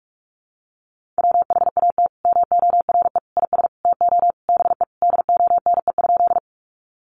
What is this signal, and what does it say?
Morse code, heard as a keyed tone, about 44 wpm, 700 Hz. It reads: WHAT MORE IS TO BE DONE?
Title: Sentences from Top 500 Words 45wpm